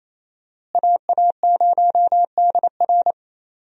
Morse code, heard as a keyed tone, about 28 wpm, 700 Hz. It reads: AA0DR
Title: USA Call Signs - 5 Characters - Repeated 2x - 28wpm